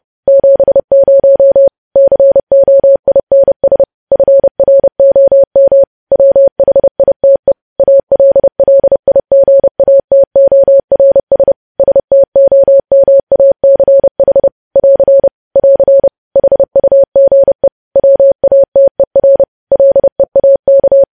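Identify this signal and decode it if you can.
Morse code, heard as a keyed tone, about 30 wpm, 550 Hz. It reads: 70 COINS FROM WHITE ALLIGATORS STOMACH <AR> <AR> HUGE WATER LEAK
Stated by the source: Morse news headlines sent at 30.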